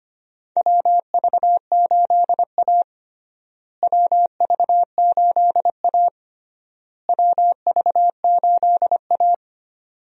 Morse code, heard as a keyed tone, about 25 wpm, 700 Hz. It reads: WV8A WV8A WV8A
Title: USA Call Signs - 4 Characters - Repeated 3x - 25wpm